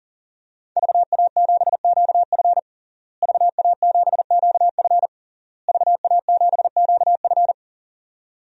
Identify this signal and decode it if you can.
Morse code, heard as a keyed tone, about 40 wpm, 700 Hz. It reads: VA7QF VA7QF VA7QF
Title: DXCC-100 Call Signs - 5 Characters - Repeated 3x - 40wpm